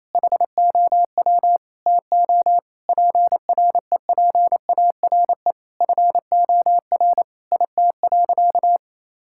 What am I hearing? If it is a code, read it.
Morse code, heard as a keyed tone, about 27 wpm, 700 Hz. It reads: HOW TO PREPARE FOR IT.
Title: Sentences from Top 1000 Words - Review Part 6 - 28wpm